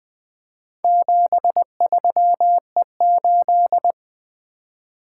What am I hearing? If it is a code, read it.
Morse code, heard as a keyed tone, about 20 wpm, 700 Hz. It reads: 73E8
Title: Four Letter-Number Combinations - 20wpm